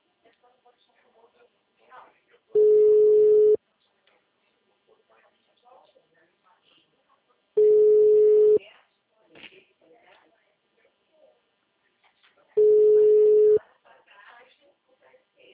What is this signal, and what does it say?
Ringback tone (Europe, 425 Hz), with speech.
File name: out- -2001- - - .56559